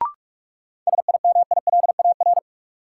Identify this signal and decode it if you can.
Morse code, heard as a keyed tone, about 44 wpm, 700 Hz. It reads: SIMILAR
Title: Sentences from Top 1000 Words - Review Part 5 - New Word Drill - 45wpm